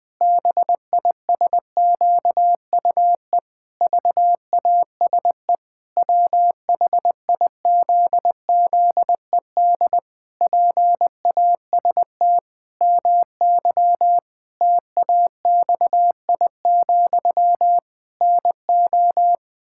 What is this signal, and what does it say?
Morse code, heard as a keyed tone, about 21 wpm, 700 Hz. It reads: BISQUE VASE WHIZZED PAST MY TAXI, NO
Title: Pangram Challenge v2 - 20wpm